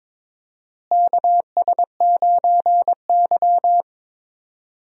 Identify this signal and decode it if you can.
Morse code, heard as a keyed tone, about 22 wpm, 700 Hz. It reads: KS9Y
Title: USA Call Signs - 4 Characters - Repeated 3x - 22wpm